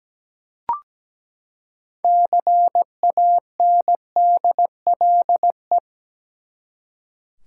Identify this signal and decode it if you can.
Morse code, heard as a keyed tone, about 17 wpm, 700 Hz. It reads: CANDLE
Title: Six Letter Words - 17wpm